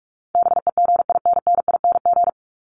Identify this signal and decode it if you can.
Morse code, heard as a keyed tone, about 46 wpm, 700 Hz. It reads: BEGINNING
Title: Top 501-750 Words 45wpm